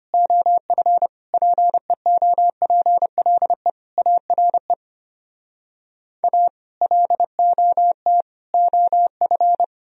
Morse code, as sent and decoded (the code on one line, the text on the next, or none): --- ..-. / .--. --- .--. .-.. / .- .-. / .- / .-.. --- - / --- ..-.
OF PEOPLE ARE A LOT OF